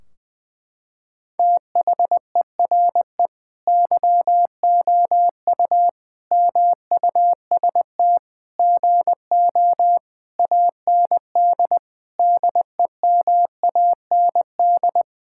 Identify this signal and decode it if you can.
Morse code, heard as a keyed tone, about 20 wpm, 700 Hz. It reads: THERE YOU MUST GO AND DEMAND
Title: Sentences from Top 1600 Words 20wpm